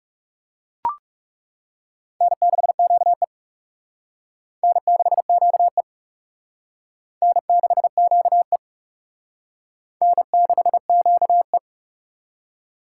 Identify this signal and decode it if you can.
Morse code, heard as a keyed tone, about 36 wpm, 700 Hz. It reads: N6QE N6QE N6QE N6QE
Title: Speed Racing - USA Call Signs 30wpm